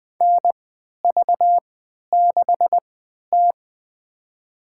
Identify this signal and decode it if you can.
Morse code, heard as a keyed tone, about 20 wpm, spaced as for 11 wpm, 700 Hz. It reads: NV6T
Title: USA Call Signs - 4 Characters - 3x Character Spacing - 20wpm